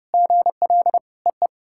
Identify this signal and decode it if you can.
Morse code, heard as a keyed tone, about 32 wpm, 700 Hz. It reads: GL EE
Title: SOTA QSO Practice 30wpm